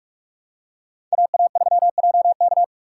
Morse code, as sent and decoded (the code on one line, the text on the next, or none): .- .- ...-- .--- -.-
AA3JK